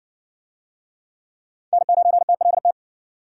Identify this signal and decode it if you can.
Morse code, heard as a keyed tone, about 53 wpm, 700 Hz. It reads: N9TLT